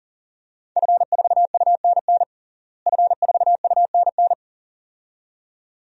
Morse code, heard as a keyed tone, about 40 wpm, 700 Hz. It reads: F4UNN F4UNN
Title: DXCC-100 Call Signs - 5 Characters - Repeated 2x - 40wpm